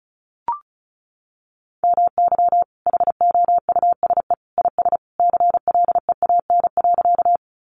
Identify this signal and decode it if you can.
Morse code, heard as a keyed tone, about 34 wpm, 700 Hz. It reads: MY HOUSE IS CLEAN.
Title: Sentences from Top 1100 Words 35wpm